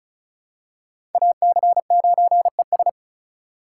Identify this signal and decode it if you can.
Morse code, heard as a keyed tone, about 36 wpm, 700 Hz. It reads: AC9ES
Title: USA Call Signs - 5 Characters - 3x Character Spacing - 35wpm